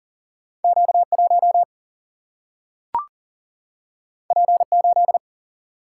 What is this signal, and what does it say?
Morse code, heard as a keyed tone, about 40 wpm, 700 Hz. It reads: Q 1 P 8
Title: Single-Letter Single-Number 40wpm